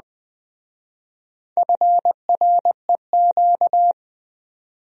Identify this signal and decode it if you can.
Morse code, heard as a keyed tone, about 21 wpm, 700 Hz. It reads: FREQ
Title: Speed-Racing Abbreviations - 17wpm